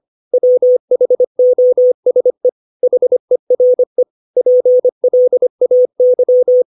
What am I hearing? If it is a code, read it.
Morse code, heard as a keyed tone, about 25 wpm, 500 Hz. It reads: WHOSE HERE PLAY
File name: 632 words / 632 Words 25WPM